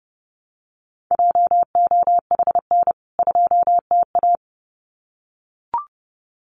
Morse code, heard as a keyed tone, about 30 wpm, 700 Hz. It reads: JOHN 2TA
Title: CWT Exchange 30wpm